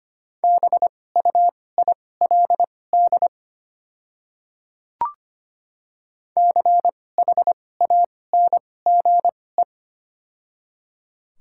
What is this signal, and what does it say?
Morse code, heard as a keyed tone, about 25 wpm, spaced as for 18 wpm, 700 Hz. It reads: BUILD CHANGE
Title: 2x Character Spacing - Top 200 Words - 25wpm